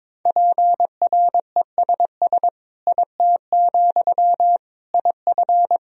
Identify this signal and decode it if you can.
Morse code, heard as a keyed tone, about 22 wpm, 700 Hz. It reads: PRESS IT, IF
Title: Sentences from Top 1000 Words - Review Part 1 - 22wpm